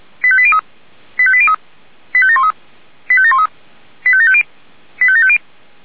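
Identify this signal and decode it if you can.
Five-tone selective call (ZVEI), heard as quick runs of five tones, tones of about 70 ms. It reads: 85803 85803 85713 85713 85750 85750
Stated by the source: ZVEI five-tone call received on SDR.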